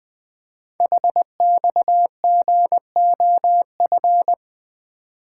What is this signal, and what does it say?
Morse code, heard as a keyed tone, about 20 wpm, 700 Hz. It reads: HXGOF
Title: Code Groups - 5 Letters - 20wpm